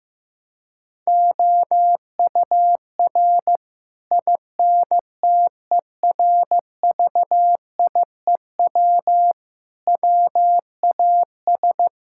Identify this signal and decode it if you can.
Morse code, heard as a keyed tone, about 15 wpm, 700 Hz. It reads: OUR INTERVIEW WAS
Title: Sentences from Top 1600 Words 15wpm